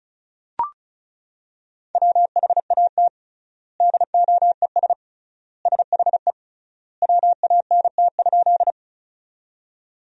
Morse code, heard as a keyed tone, about 36 wpm, 700 Hz. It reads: WHAT DOES SHE WANT?
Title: 3x Word-Spacing - Sentences from Top 200 Words - 35wpm